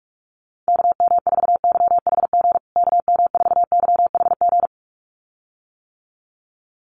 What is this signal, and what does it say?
Morse code, heard as a keyed tone, about 47 wpm, 700 Hz. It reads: KM4YHG KM4YHG
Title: USA Call Signs Repeated 2x 45wpm